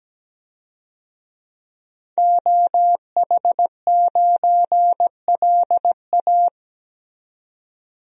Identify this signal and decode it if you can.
Morse code, heard as a keyed tone, about 17 wpm, 700 Hz. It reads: OH9LA
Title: DXCC-100 Call Signs - 5 Characters - 17wpm